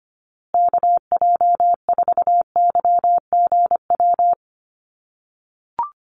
Morse code, heard as a keyed tone, about 25 wpm, 700 Hz. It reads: KJ4YGW
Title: USA Call Signs Repeated 3x 25wpm